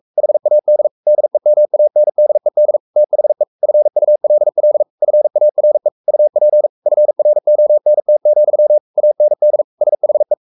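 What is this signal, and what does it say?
Morse code, heard as a keyed tone, about 44 wpm, 600 Hz. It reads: HAD DEMANDED THE FULL FARE UP FRONT, AND SHE
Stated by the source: Morse code at 43 wpm sending Ken Follett's novel Never in English, from CW med Gnister.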